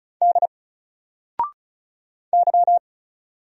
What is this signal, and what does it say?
Morse code, heard as a keyed tone, about 36 wpm, 700 Hz. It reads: D Y
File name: Single-Letter-Number-35wpm